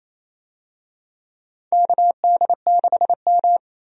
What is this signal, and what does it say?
Morse code, heard as a keyed tone, about 28 wpm, 700 Hz. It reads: KD6M
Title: USA Call Signs - 4 Characters - 28wpm